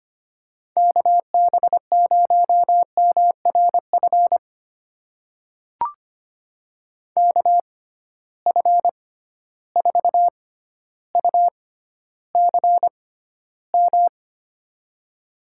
Morse code, heard as a keyed tone, about 25 wpm, spaced as for 13 wpm, 700 Hz. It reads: KB0MRF KF4UCM
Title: USA Call Signs - 6 Characters - 6x Character Spacing - 25wpm